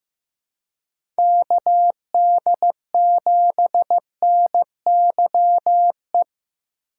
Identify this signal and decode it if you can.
Morse code, heard as a keyed tone, about 15 wpm, 700 Hz. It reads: KD7NYE
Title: USA Call Signs - 6 Characters - 2x Character Spacing - 15wpm